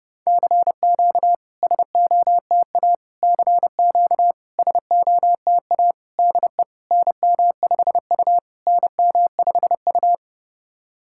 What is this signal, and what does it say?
Morse code, heard as a keyed tone, about 30 wpm, 700 Hz. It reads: CQ SOTA CQ SOTA DE NM5U NM5U
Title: SOTA QSO Practice 30wpm